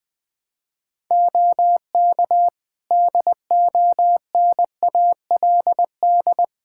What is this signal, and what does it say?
Morse code, heard as a keyed tone, about 21 wpm, 700 Hz. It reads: OK DONALD
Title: Practice QSO - Exchange 3 - More Info - 20wpm